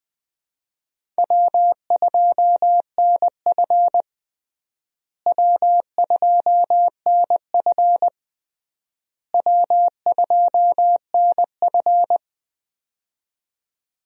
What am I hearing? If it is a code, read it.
Morse code, heard as a keyed tone, about 21 wpm, 700 Hz. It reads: W2NF W2NF W2NF
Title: USA Call Signs - 4 Characters - Repeated 3x - 20wpm